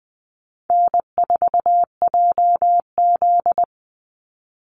Morse code, heard as a keyed tone, about 20 wpm, 700 Hz. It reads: N4JZ
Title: USA Call Signs 20wpm